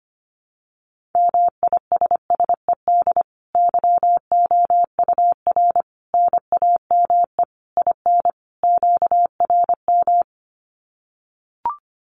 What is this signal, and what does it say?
Morse code, heard as a keyed tone, about 25 wpm, 700 Hz. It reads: MISSED YOUR NAME IN QRM
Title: Common QSO Phrases 25wpm